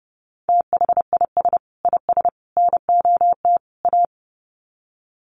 Morse code, heard as a keyed tone, about 28 wpm, 700 Hz. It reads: THIS IS NOT A
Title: Sets of 4 Words 30wpm